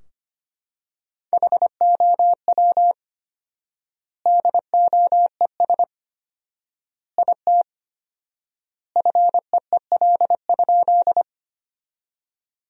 Morse code, heard as a keyed tone, about 25 wpm, 700 Hz. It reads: HOW DOES IT FEEL?
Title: 4x Word-Spacing - Sentences from Top 400 Words - 25wpm